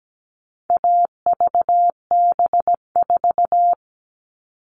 Morse code, heard as a keyed tone, about 17 wpm, 700 Hz. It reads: AVB4
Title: All Cut Numbers - 4-Digit 17wpm